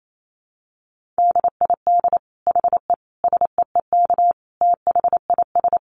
Morse code, heard as a keyed tone, about 28 wpm, 700 Hz. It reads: DID HE SEEK THIS
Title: Dit Challenge 28wpm